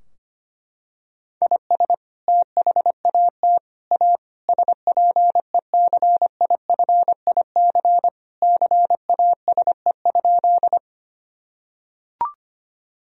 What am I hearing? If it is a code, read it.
Morse code, heard as a keyed tone, about 25 wpm, 700 Hz. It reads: IS THAT A SPECIFIC CASE?
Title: Sentences from Top 1000 Words - Review Part 3 - 25wpm